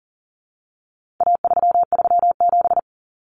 Morse code, heard as a keyed tone, about 40 wpm, 700 Hz. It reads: A337
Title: Common Cut Numbers - 4-Digit 40wpm